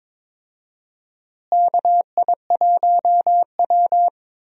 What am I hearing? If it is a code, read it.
Morse code, heard as a keyed tone, about 22 wpm, 700 Hz. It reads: KI1W
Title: USA Call Signs - 4 Characters - Repeated 3x - 22wpm